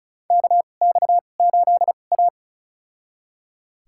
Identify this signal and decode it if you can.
Morse code, heard as a keyed tone, about 36 wpm, spaced as for 25 wpm, 700 Hz. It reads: KX8A
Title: USA Call Signs - 4 Characters - 2x Character Spacing - 35wpm